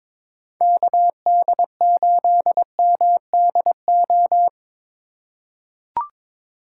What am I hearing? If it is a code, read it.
Morse code, heard as a keyed tone, about 22 wpm, 700 Hz. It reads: KD8MDO
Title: USA Call Signs - 4x Character Spacing - 22wpm